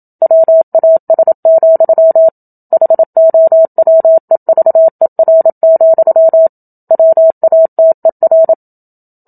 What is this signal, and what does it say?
Morse code, heard as a keyed tone, about 27 wpm, 650 Hz. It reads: WAS, HOWEVER, WATER
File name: Book Two / 01-Chapter 1 - 27 WPM